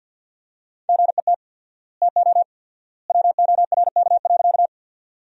Morse code, heard as a keyed tone, about 50 wpm, 700 Hz. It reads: GET TO WORK.
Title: 4x Word-Spacing - Sentences from Top 200 Words - 50wpm